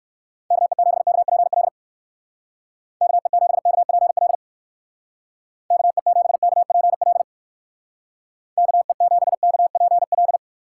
Morse code, heard as a keyed tone, about 56 wpm, 700 Hz. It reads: KE7KPL KE7KPL KE7KPL KE7KPL
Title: Speed Racing - USA Call Signs 45wpm